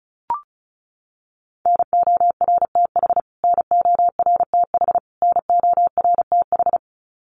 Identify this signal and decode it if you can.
Morse code, heard as a keyed tone, about 34 wpm, 700 Hz. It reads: NORTH NORTH NORTH
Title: Top 500 Words Repeated 35wpm